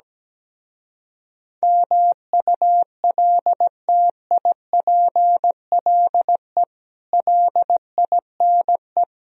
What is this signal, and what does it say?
Morse code, heard as a keyed tone, about 17 wpm, 700 Hz. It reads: MULTIPLE LINE
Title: Speed-Racing - Sets of 5 Words 17wpm